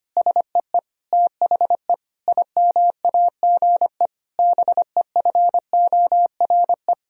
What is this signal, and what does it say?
Morse code, heard as a keyed tone, about 25 wpm, 700 Hz. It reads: SEE THE IMAGE BEFORE
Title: Sentences from Top 1000 Words - Review Part 4 - 25wpm